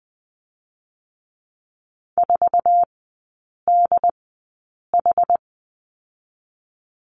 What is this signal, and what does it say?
Morse code, heard as a keyed tone, about 20 wpm, spaced as for 8 wpm, 690 Hz. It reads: 4DH
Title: 087 Lesson 15e - Number 5 - Three Characters 20wpm